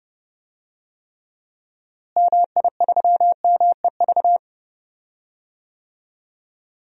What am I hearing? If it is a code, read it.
Morse code, heard as a keyed tone, about 30 wpm, 700 Hz. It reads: MI3MEV
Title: DXCC-100 Call Signs - 6 Characters - Repeated 2x - 30wpm